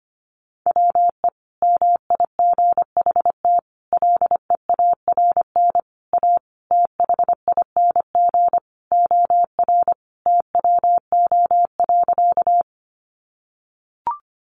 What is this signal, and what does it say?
Morse code, heard as a keyed tone, about 25 wpm, 700 Hz. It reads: WE MIGHT LEARN A THING OR TWO.
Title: Sentences from Top 300 Words 25wpm